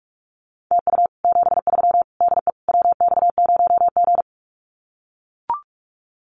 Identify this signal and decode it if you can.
Morse code, heard as a keyed tone, about 44 wpm, 700 Hz. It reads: TU 73 DE WX0G
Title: Common QSO Phrases Encore 45wpm